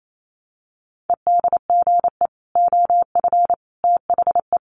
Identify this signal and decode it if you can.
Morse code, heard as a keyed tone, about 28 wpm, 700 Hz. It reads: EDGE OF THE
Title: Sets of 3 Words 28wpm